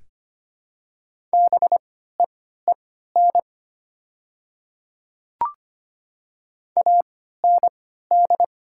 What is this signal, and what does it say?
Morse code, heard as a keyed tone, about 25 wpm, spaced as for 14 wpm, 700 Hz. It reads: BEEN AND
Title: 3x Character Spacing - Top 100 Words - 25wpm